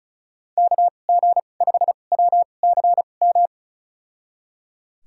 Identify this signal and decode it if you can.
Morse code, heard as a keyed tone, about 36 wpm, spaced as for 25 wpm, 700 Hz. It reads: KG5WCM
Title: USA Call Signs - 6 Characters - 2x Character Spacing - 35wpm